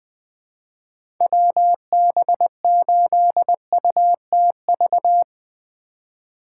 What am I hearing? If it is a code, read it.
Morse code, heard as a keyed tone, about 21 wpm, 700 Hz. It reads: WB8UTV
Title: USA Call Signs - 4x Character Spacing - 20wpm